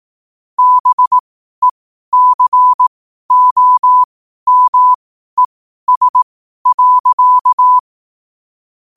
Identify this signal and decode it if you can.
Morse code, heard as a keyed tone, about 18 wpm, spaced as for 12 wpm, 1000 Hz. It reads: BECOMES.
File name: qotdfiles / 12wpm